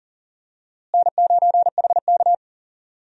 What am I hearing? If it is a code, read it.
Morse code, heard as a keyed tone, about 40 wpm, 700 Hz. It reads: N9HK
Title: USA Call Signs - 4 Characters - Repeated 3x - 40wpm